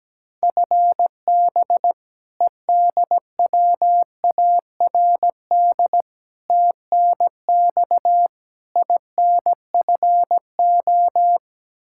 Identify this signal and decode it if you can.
Morse code, heard as a keyed tone, about 17 wpm, 700 Hz. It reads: FB EDWARD TNX INFO
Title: Practice QSO - Exchange 3 - More Info - 17wpm